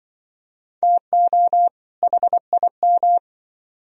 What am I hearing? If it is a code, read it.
Morse code, heard as a keyed tone, about 24 wpm, 700 Hz. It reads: TO HIM
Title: Speed-Racing - Sets of 2 Words 20wpm